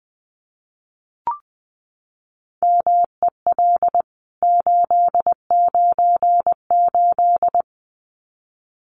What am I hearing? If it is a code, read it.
Morse code, heard as a keyed tone, about 20 wpm, 700 Hz. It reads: MEL 898
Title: CWT Exchange 20wpm